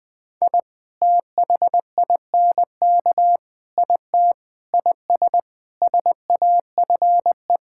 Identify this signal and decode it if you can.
Morse code, heard as a keyed tone, about 20 wpm, 700 Hz. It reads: I THINK IT IS SAFE
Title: Sets of 8 Words - 20wpm